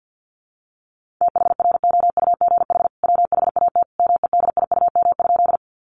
Morse code, heard as a keyed tone, about 50 wpm, 700 Hz. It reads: THROUGH WHAT MEDIUM?
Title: Sentences from Top 900 Words 50wpm